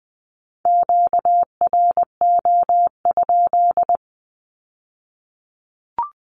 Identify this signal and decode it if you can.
Morse code, heard as a keyed tone, about 20 wpm, 700 Hz. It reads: QRO?
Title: Q-Codes with Spoken Meaning - 20wpm